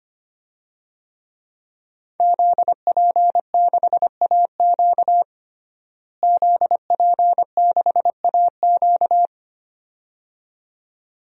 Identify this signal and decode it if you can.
Morse code, heard as a keyed tone, about 25 wpm, 700 Hz. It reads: ZP6AQ ZP6AQ
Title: DXCC-100 Call Signs - 5 Characters - Repeated 2x - 25wpm